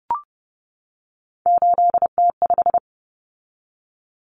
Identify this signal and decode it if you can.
Morse code, heard as a keyed tone, about 30 wpm, 700 Hz. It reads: 8T5
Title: Common Cut Numbers - 3-Digit 30wpm